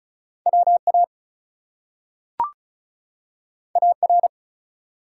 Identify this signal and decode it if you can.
Morse code, heard as a keyed tone, about 36 wpm, 700 Hz. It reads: WA AR
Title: US States and Canadian Provinces 35wpm